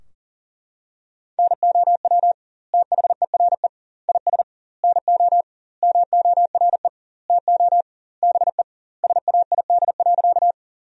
Morse code, heard as a keyed tone, about 40 wpm, 700 Hz. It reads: NOW THERE IS NO MORE TO BE SAID.
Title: 2x Word-Spacing - Sentences from Top 100 Words - 40wpm